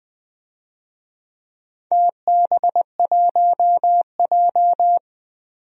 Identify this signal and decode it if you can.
Morse code, heard as a keyed tone, about 20 wpm, 700 Hz. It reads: TB1J
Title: DXCC-100 Call Signs - 4 Characters - 20wpm